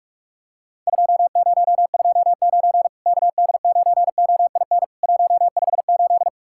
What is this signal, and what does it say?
Morse code, heard as a keyed tone, about 44 wpm, 700 Hz. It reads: 2029 KD9OIN 158
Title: Worked All Europe DX Contest - QTC Exchange - 45wpm